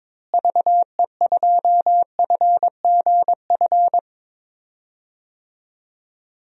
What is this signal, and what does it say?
Morse code, heard as a keyed tone, about 22 wpm, 700 Hz. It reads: VE2FGF
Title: DXCC-100 Call Signs - 6 Characters - Repeated 2x - 22wpm